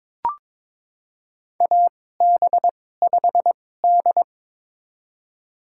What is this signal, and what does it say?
Morse code, heard as a keyed tone, about 22 wpm, spaced as for 16 wpm, 700 Hz. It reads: AB5D
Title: USA Call Signs - 4 Characters - 2x Character Spacing - 22wpm